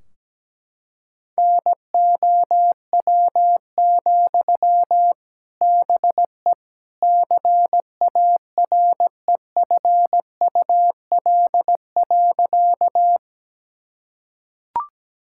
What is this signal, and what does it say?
Morse code, heard as a keyed tone, about 17 wpm, 700 Hz. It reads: NOW, BE CAREFUL.
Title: Sentences from Top 1600 Words 17wpm